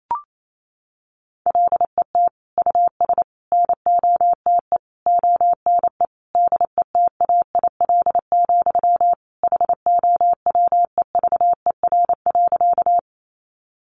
Morse code, heard as a keyed tone, about 28 wpm, 700 Hz. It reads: LET US NOTE ONE DETAIL, HOWEVER.
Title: Sentences from Top 1500 Words 28wpm